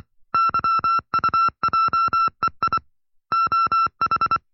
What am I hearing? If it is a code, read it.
Morse code, heard as a keyed tone, about 24 wpm, 1340 Hz. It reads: YUJEI OH
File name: FreeFiles / CW-practice / 20WPM-side1